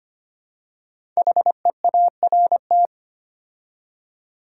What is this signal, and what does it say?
Morse code, heard as a keyed tone, about 25 wpm, 700 Hz. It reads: HEART
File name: 5 Letter Words - First 3 Letters Given - 25wpm